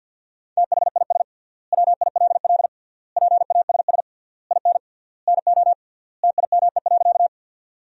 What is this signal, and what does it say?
Morse code, heard as a keyed tone, about 50 wpm, 700 Hz. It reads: THIS WILL PASS IN NO TIME.
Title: 3x Word-Spacing - Sentences from Top 400 Words - 50wpm